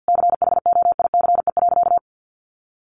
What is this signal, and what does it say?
Morse code, heard as a keyed tone, about 49 wpm, 700 Hz. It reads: CHOICE.
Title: Sentences from Top 900 Words 50wpm